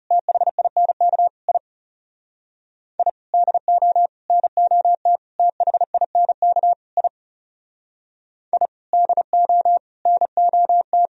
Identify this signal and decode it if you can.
Morse code, heard as a keyed tone, about 37 wpm, 700 Hz. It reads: THINK I I DO NOT THINK I I DO NOT
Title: Speed-Racing - Sets of 5 Words 30wpm